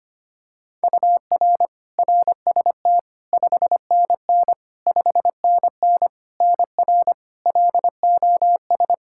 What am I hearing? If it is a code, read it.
Morse code, heard as a keyed tone, about 25 wpm, 700 Hz. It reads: UR RST 5NN 5NN NR LOS
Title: K1USN SST QSO Practice 25wpm